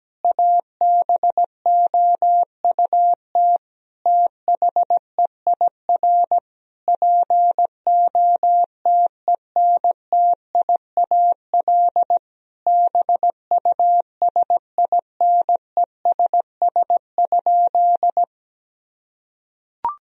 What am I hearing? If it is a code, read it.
Morse code, heard as a keyed tone, about 17 wpm, 700 Hz. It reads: ABOUT THEIR POTENTIAL BUSINESS?
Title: Sentences from Top 1000 Words - Review Part 4 - 17wpm